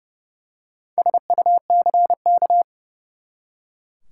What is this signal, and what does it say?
Morse code, heard as a keyed tone, about 31 wpm, 700 Hz. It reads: SUCK
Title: Four Letter Words - 30wpm